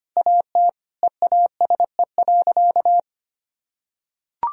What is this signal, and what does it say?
Morse code, heard as a keyed tone, about 25 wpm, 700 Hz. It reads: AT EASE.
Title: Sentences from Top 1000 Words - Review Part 1 - 25wpm